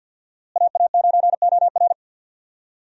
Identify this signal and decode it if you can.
Morse code, heard as a keyed tone, about 47 wpm, 700 Hz. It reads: AA9OR
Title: USA Call Signs - 5 Characters - Repeated 3x - 50wpm